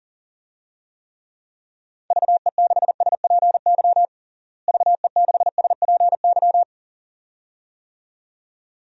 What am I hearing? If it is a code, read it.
Morse code, heard as a keyed tone, about 40 wpm, 700 Hz. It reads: VE6SPY VE6SPY
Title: DXCC-100 Call Signs - 6 Characters - Repeated 2x - 40wpm